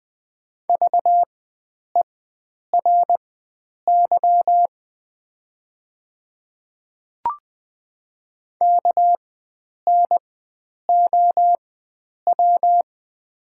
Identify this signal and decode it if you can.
Morse code, heard as a keyed tone, about 20 wpm, spaced as for 9 wpm, 700 Hz. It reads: VERY KNOW
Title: 4x Character Spacing - Top 300 Words - 20wpm